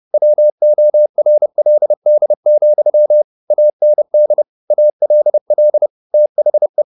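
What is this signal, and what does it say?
Morse code, heard as a keyed tone, about 30 wpm, 600 Hz. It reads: WORLD, AND ALL THE
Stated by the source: Morse code at 30 wpm, the novel Never by Ken Follett in English, from CW med Gnister.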